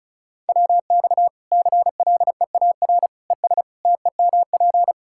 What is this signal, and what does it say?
Morse code, heard as a keyed tone, about 34 wpm, 700 Hz. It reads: WX CLEAR ES TEMP
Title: K1USN SST QSO Practice 35wpm